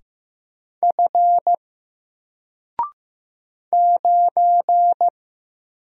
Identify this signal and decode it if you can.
Morse code, heard as a keyed tone, about 15 wpm, 700 Hz. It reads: F 9
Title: Single Character (Letter:Number:Punctuation:Prosign) 15wpm